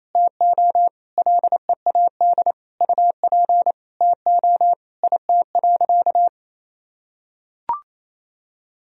Morse code, heard as a keyed tone, about 27 wpm, 700 Hz. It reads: TO LEAD UP TO IT.
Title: Sentences from Top 1000 Words - Review Part 2 - 28wpm